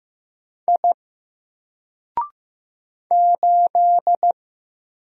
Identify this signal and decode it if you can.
Morse code, heard as a keyed tone, about 15 wpm, 700 Hz. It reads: I 8
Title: Single Character (Letter:Number:Punctuation:Prosign) 15wpm